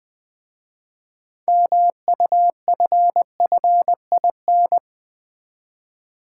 Morse code, heard as a keyed tone, about 20 wpm, 700 Hz. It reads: MUFFIN
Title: Easy Words with Hints - 20wpm